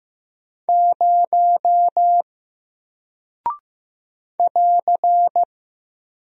Morse code, heard as a keyed tone, about 15 wpm, 700 Hz. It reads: 0 <AR>
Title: Single Character (Letter:Number:Punctuation:Prosign) Alt 15wpm